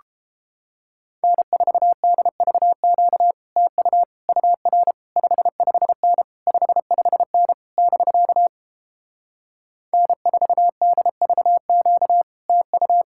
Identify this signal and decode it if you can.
Morse code, heard as a keyed tone, about 29 wpm, 700 Hz. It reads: N4DVQ TU UR 55N 55N <BK> N4DVQ TU